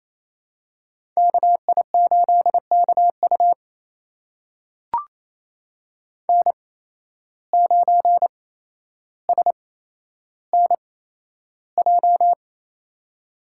Morse code, heard as a keyed tone, about 28 wpm, spaced as for 12 wpm, 700 Hz. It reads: KI8KU N9SNJ
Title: USA Call Signs - 8x Character Spacing - 28wpm